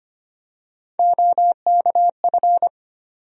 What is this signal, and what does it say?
Morse code, heard as a keyed tone, about 25 wpm, 700 Hz. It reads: OKF
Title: USA Call Sign Postfixes - 3 Letters 25wpm